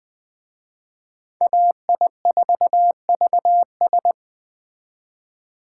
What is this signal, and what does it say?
Morse code, heard as a keyed tone, about 20 wpm, 700 Hz. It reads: AI4VS